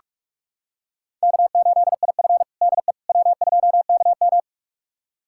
Morse code, heard as a keyed tone, about 46 wpm, 700 Hz. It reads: K8IF DE W2KM
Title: USA Callsign DE Callsign - 4 Characters - 45wpm